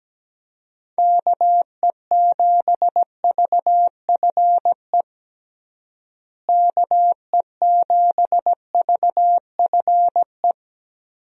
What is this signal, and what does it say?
Morse code, heard as a keyed tone, about 17 wpm, 700 Hz. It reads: KE7VFE KE7VFE
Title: USA Call Signs - 6 Characters - Repeated 3x - 17wpm